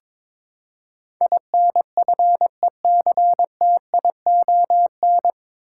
Morse code, heard as a keyed tone, about 22 wpm, 700 Hz. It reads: INFECTION
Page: Morse Code Ninja practice 22wpm